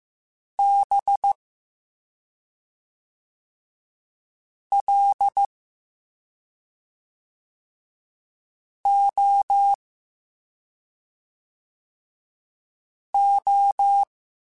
Morse code, heard as a keyed tone, about 15 wpm, 800 Hz. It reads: B L O O